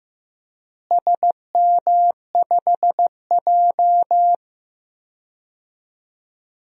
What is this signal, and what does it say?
Morse code, heard as a keyed tone, about 15 wpm, 700 Hz. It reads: SM5J
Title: DXCC-100 Call Signs - 4 Characters - Repeated 3x - 15wpm